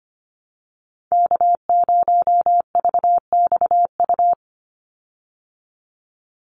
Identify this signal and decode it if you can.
Morse code, heard as a keyed tone, about 25 wpm, 700 Hz. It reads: K0VXU
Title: Call Signs Repeated 3x - 25wpm